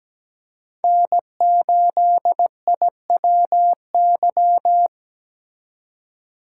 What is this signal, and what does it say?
Morse code, heard as a keyed tone, about 17 wpm, 700 Hz. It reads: N8IWY
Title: K1USN SST QSO Practice - 17wpm